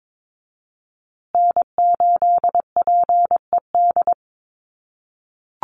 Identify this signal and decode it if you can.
Morse code, heard as a keyed tone, about 22 wpm, 700 Hz. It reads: N8PED
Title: USA Call Signs Repeated 2x 22wpm